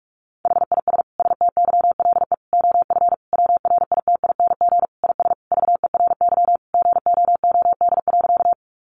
Morse code, heard as a keyed tone, about 44 wpm, 700 Hz. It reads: HIS STYLE OF WRITING IS VERY GOOD.
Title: Sentences from Top 800 Words 45wpm